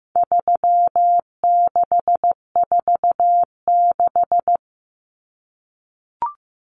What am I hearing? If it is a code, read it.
Morse code, heard as a keyed tone, about 15 wpm, 700 Hz. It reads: 3646
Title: Common Cut Numbers - 4-Digit 15wpm